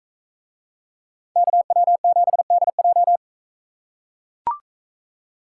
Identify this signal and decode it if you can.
Morse code, heard as a keyed tone, about 41 wpm, 700 Hz. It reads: KW7DJ